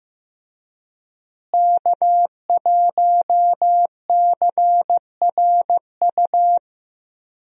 Morse code, heard as a keyed tone, about 15 wpm, 700 Hz. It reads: K1CRU
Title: USA Call Signs - 5 Characters - Repeated 3x - 15wpm